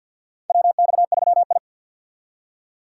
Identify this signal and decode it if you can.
Morse code, heard as a keyed tone, about 52 wpm, 700 Hz. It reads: WX3I
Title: USA Call Signs - 4 Characters - 2x Character Spacing - 50wpm